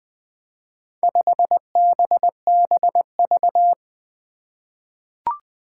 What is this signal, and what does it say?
Morse code, heard as a keyed tone, about 20 wpm, 700 Hz. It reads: 5BBV
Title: Four Challenging Letter-Number Combinations 20wpm